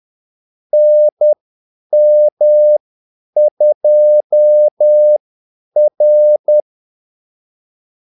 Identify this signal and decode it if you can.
Morse code, heard as a keyed tone, about 10 wpm, spaced as for 8 wpm, 600 Hz. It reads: NM2R
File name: Callsigns-10-8wpm